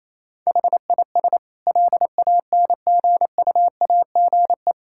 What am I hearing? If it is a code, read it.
Morse code, heard as a keyed tone, about 28 wpm, 700 Hz. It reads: HIS LANGUAGE
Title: Sentences from Top 1600 Words 28wpm